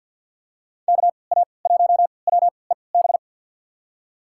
Morse code, heard as a keyed tone, about 53 wpm, spaced as for 29 wpm, 700 Hz. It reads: KA1WEB
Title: USA Call Signs - 6 Characters - 3x Character Spacing - 50wpm